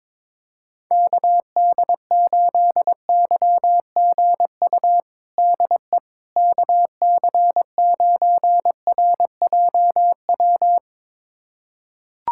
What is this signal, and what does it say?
Morse code, heard as a keyed tone, about 22 wpm, 700 Hz. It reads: KD8YGU DE KC9RJW
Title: USA Callsign DE Callsign - 6 Characters - 22wpm